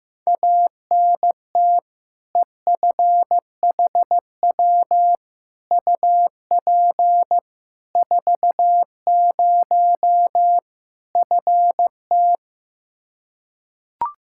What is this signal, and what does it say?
Morse code, heard as a keyed tone, about 15 wpm, 700 Hz. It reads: ANT EFHW UP 40 FT
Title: Practice QSO - Exchange 2 - Info - 15wpm